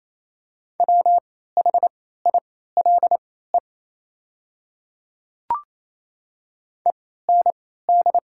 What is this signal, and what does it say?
Morse code, heard as a keyed tone, about 28 wpm, spaced as for 16 wpm, 700 Hz. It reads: WHILE END
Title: 3x Character Spacing - Top 300 Words - 28wpm